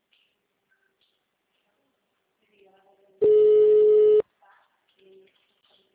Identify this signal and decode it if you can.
Ringback tone (Europe, 425 Hz), with speech.
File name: out- -2011- - - .241